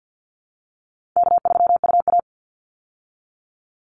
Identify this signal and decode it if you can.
Morse code, heard as a keyed tone, about 50 wpm, 700 Hz. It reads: K3UA
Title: Call Signs Repeated 3x - 50wpm